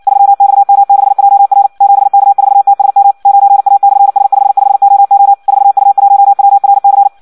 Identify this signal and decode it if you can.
Morse code, heard as a keyed tone, about 56 wpm, 800 Hz. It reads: 3XM6JR 7K4TSM 9A/S55OO 4U1WRC
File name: Audio practica CW / qrq-tx-58 2